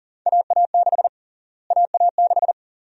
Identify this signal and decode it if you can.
Morse code, heard as a keyed tone, about 40 wpm, 700 Hz. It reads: AA6 AA6
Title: USA Call Sign Prefixes - Repeated 2x - 40wpm